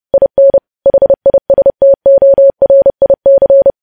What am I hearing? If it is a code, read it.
Morse code, heard as a keyed tone, about 30 wpm, 550 Hz. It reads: IN HISTORIC
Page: news headlines in morse code at 30 wpm